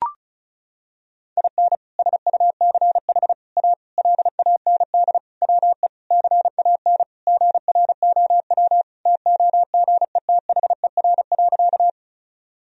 Morse code, heard as a keyed tone, about 36 wpm, 700 Hz. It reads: IN SUCH A LAND WE CAN GROW TOGETHER.
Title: Sentences from Top 1000 Words - Review Part 1 - 35wpm